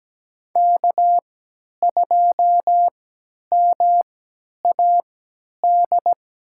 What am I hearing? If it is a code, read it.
Morse code, heard as a keyed tone, about 17 wpm, spaced as for 10 wpm, 700 Hz. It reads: K2MAD
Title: USA Call Signs - 5 Characters - 3x Character Spacing - 17wpm